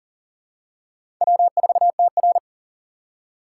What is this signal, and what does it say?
Morse code, heard as a keyed tone, about 37 wpm, 700 Hz. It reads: W4TR
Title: USA Call Signs - 4 Characters - 3x Character Spacing - 40wpm